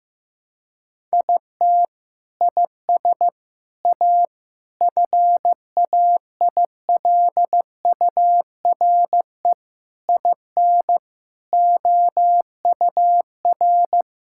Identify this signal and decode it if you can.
Morse code, heard as a keyed tone, about 15 wpm, 700 Hz. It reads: IT IS A FAILURE IN OUR
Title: Sentences from Top 1000 Words - Review Part 6 - 15wpm